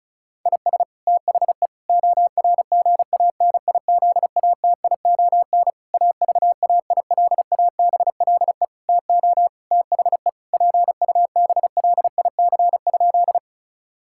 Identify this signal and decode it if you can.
Morse code, heard as a keyed tone, about 36 wpm, 700 Hz. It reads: IS THE ORGANIZATION AVAILABLE TO THE PUBLIC?
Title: Sentences from Top 1000 Words - Review Part 5 - 35wpm